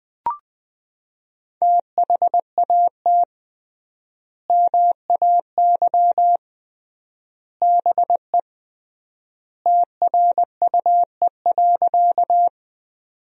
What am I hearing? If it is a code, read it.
Morse code, heard as a keyed tone, about 20 wpm, 700 Hz. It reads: THAT MAY BE TRUE.
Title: 3x Word-Spacing - Sentences from Top 400 Words - 20wpm